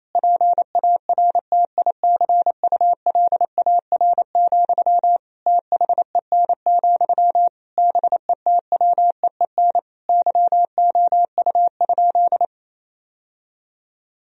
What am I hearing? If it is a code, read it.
Morse code, heard as a keyed tone, about 28 wpm, 700 Hz. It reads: PARTICULAR, THEN, BETWEEN YOU?
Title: Sentences from Top 1000 Words - Review Part 4 - 28wpm